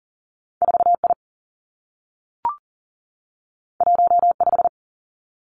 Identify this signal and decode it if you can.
Morse code, heard as a keyed tone, about 39 wpm, 700 Hz. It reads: 4 I 1 5
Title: Two Challenging Letter-Number Combinations 40wpm